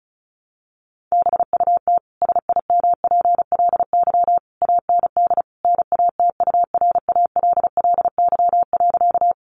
Morse code, heard as a keyed tone, about 34 wpm, 700 Hz. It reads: BUT SIMPLY AND NATURALLY.
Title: Sentences from Top 1300 Words 35wpm